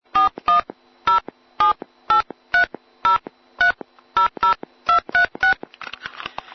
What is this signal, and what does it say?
DTMF, heard as telephone touch tones, tones of about 120 ms. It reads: 4107830300236